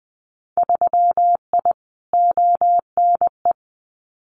Morse code, heard as a keyed tone, about 20 wpm, 700 Hz. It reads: 3I ONE
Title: Winter Field Day Exchanges Encore 20wpm